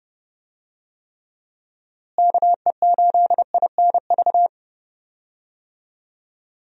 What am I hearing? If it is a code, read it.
Morse code, heard as a keyed tone, about 32 wpm, 700 Hz. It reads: KE8INV
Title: USA Call Signs - 6 Characters - Repeated 3x - 30wpm